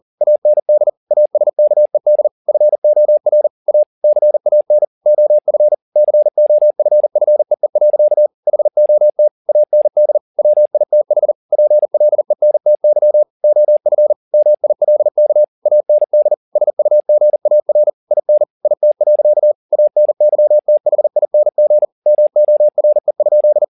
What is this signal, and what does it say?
Morse code, heard as a keyed tone, about 40 wpm, 600 Hz. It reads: AND ASKED FOR A CAN OF COFFEE. HOT AND WITH PLENTY OF MILK AND SUGAR IN IT. ANYTHING MORE?
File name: the old man and the sea 2 40wpm